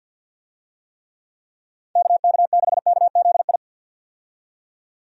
Morse code, heard as a keyed tone, about 53 wpm, 700 Hz. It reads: KK6KZI
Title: USA Call Signs - 6 Characters - 50wpm